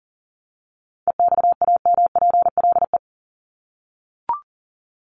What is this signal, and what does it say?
Morse code, heard as a keyed tone, about 40 wpm, 700 Hz. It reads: EXAMPLE
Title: Top 500 Words 40wpm